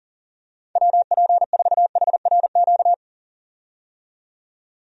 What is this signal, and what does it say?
Morse code, heard as a keyed tone, about 40 wpm, 700 Hz. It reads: WP4HRQ
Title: USA Call Signs - 6 Characters - 40wpm